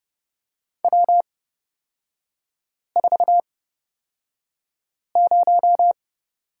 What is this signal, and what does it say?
Morse code, heard as a keyed tone, about 30 wpm, spaced as for 5 wpm, 700 Hz. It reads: W40
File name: Lesson 38e - Letter q - Three Characters 30wpm